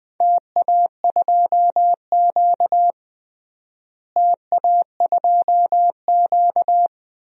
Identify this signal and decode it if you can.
Morse code, heard as a keyed tone, about 21 wpm, 700 Hz. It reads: TA2Q TA2Q
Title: DXCC-100 Call Signs - 4 Characters - Repeated 2x - 20wpm